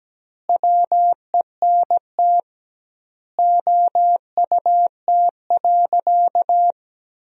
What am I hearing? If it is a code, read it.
Morse code, heard as a keyed tone, about 17 wpm, 700 Hz. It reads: WENT OUT.
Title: 2x Word-Spacing - Sentences from Top 300 Words - 17wpm